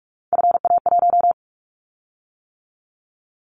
Morse code, heard as a keyed tone, about 46 wpm, 700 Hz. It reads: FA1
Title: Callsign Prefixes for Top 50 DXCC - 45wpm